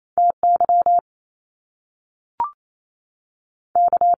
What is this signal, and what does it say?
Morse code, heard as a keyed tone, about 28 wpm, spaced as for 3 wpm, 700 Hz. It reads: T YK
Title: Top Two Letter Combinations in Words 28wpm